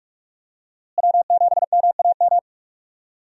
Morse code, heard as a keyed tone, about 41 wpm, 700 Hz. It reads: W7MAM
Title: USA Call Signs - 5 Characters - Repeated 3x - 45wpm